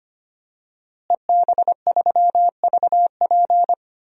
Morse code, heard as a keyed tone, about 25 wpm, 700 Hz. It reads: EB3VP
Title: DXCC-100 Call Signs - 5 Characters - Repeated 3x - 25wpm